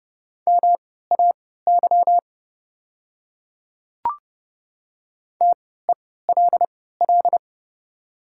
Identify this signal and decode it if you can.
Morse code, heard as a keyed tone, about 30 wpm, spaced as for 17 wpm, 700 Hz. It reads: MAY TELL
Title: 3x Character Spacing - Top 200 Words - 30wpm